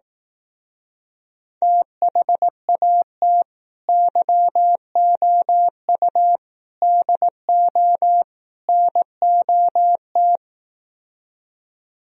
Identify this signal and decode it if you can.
Morse code, heard as a keyed tone, about 18 wpm, 700 Hz. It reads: THAT YOU DO NOT
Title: Speed-Racing - Sets of 4 Words 15wpm